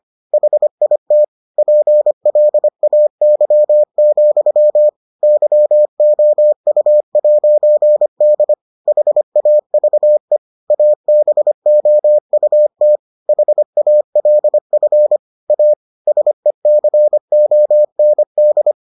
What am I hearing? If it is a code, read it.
Morse code, heard as a keyed tone, about 25 wpm, 600 Hz. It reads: HIT PLAY, YOU'D HAVE ABOUT HALF A SECOND